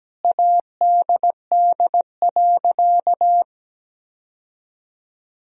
Morse code, heard as a keyed tone, about 17 wpm, 700 Hz. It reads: ADD.
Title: Sentences from Top 1000 Words - Review Part 1 - 17wpm